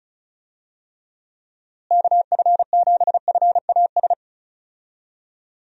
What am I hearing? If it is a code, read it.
Morse code, heard as a keyed tone, about 36 wpm, 700 Hz. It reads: KF7FAS
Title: USA Call Signs - 6 Characters - Repeated 2x - 35wpm